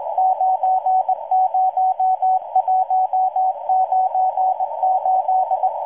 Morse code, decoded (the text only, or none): 90198